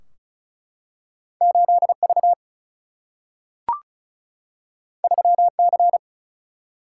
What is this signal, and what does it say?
Morse code, heard as a keyed tone, about 36 wpm, 700 Hz. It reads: 8 V 3 C
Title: Single-Letter Single-Number 35wpm